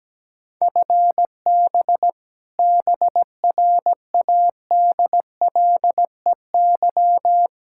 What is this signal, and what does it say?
Morse code, heard as a keyed tone, about 17 wpm, 700 Hz. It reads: FB BRADLEY
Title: Practice QSO - Exchange 3 - More Info - 17wpm